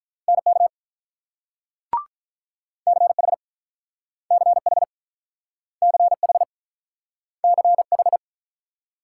Morse code, heard as a keyed tone, about 46 wpm, 700 Hz. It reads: NK CH CH CH CH